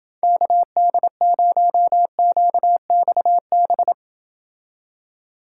Morse code, heard as a keyed tone, about 31 wpm, 700 Hz. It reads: KD0QXB